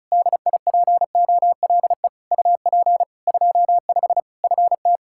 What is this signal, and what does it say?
Morse code, heard as a keyed tone, about 34 wpm, 700 Hz. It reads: DIPOLE UP 25 FT
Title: K1USN SST QSO Practice 35wpm